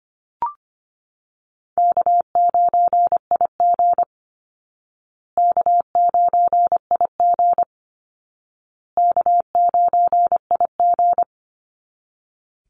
Morse code, heard as a keyed tone, about 25 wpm, 700 Hz. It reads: K9IG K9IG K9IG
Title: Easy Call Signs Repeated 3x - 25wpm